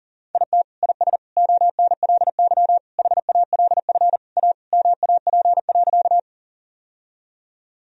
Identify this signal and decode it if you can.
Morse code, heard as a keyed tone, about 37 wpm, 700 Hz. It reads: IT IS ONLY HALF A MAP.
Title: Sentences from Top 1000 Words - Review Part 2 - 40wpm